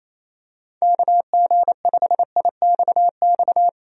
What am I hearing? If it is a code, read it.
Morse code, heard as a keyed tone, about 27 wpm, 700 Hz. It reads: KG5IXX
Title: USA Call Signs - 6 Characters - 28wpm